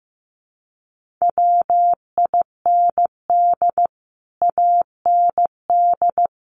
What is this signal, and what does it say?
Morse code, heard as a keyed tone, about 15 wpm, 700 Hz. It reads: WIND AND
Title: Sentences from Top 1400 Words 15wpm